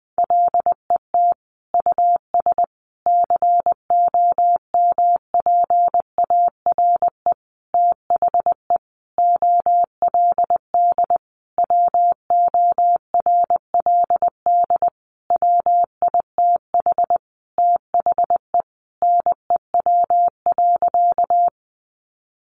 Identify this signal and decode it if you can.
Morse code, heard as a keyed tone, about 20 wpm, 700 Hz. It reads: LET US COMPARE THE OLD WORLD WITH THE NEW.
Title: Sentences from Top 1300 Words 20wpm